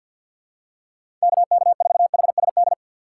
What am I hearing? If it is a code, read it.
Morse code, heard as a keyed tone, about 50 wpm, 700 Hz. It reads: KK4HSD
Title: USA Call Signs - 6 Characters - 6x Character Spacing - 50wpm